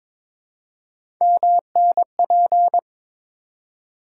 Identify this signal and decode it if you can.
Morse code, heard as a keyed tone, about 22 wpm, 700 Hz. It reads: MNP
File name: practice / 22wpm / Three Letter-Number Combinations - 22wpm